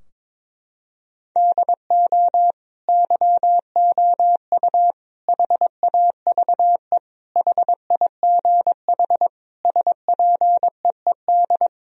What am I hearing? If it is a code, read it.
Morse code, heard as a keyed tone, about 22 wpm, 700 Hz. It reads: DO YOU HAVE HIGH SPEED
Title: Sentences from Top 1000 Words - Review Part 4 - 22wpm